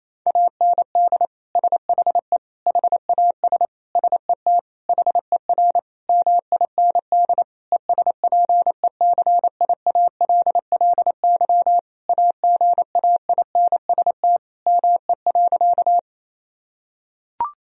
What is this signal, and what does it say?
Morse code, heard as a keyed tone, about 28 wpm, 700 Hz. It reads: AND SHE HAS SET HER MIND ESPECIALLY AGAINST ME.
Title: Sentences from Top 1000 Words - Review Part 4 - 28wpm